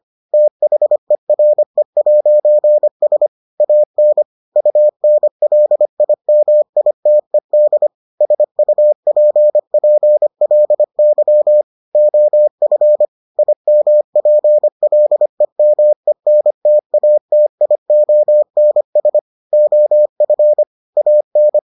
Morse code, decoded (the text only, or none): THERE'S AN UNLIMITED SUPPLY OF IMPLEMENTATIONS OF AN